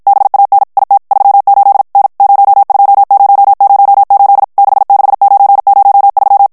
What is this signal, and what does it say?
Morse code, heard as a keyed tone, about 54 wpm, 800 Hz. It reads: BAN IT 38 N 02008 66903
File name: morse mp3 / ITALIEN texte clair 1 vitesse 50 mots minute